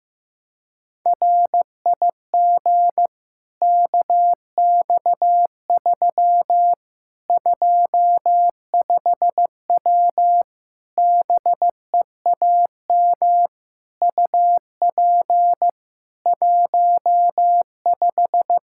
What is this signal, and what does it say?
Morse code, heard as a keyed tone, about 15 wpm, 700 Hz. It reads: RIG KX3 25W BEAM UP 15
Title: K1USN SST QSO Practice 15wpm